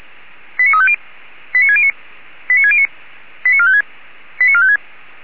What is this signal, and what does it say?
Five-tone selective call (ZVEI), heard as quick runs of five tones, tones of about 70 ms. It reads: 89350 89609 89609 89456 89456